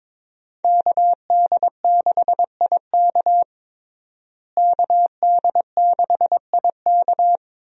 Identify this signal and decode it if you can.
Morse code, heard as a keyed tone, about 22 wpm, 700 Hz. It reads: KD6IK KD6IK
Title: USA Call Signs - 5 Characters - Repeated 2x - 22wpm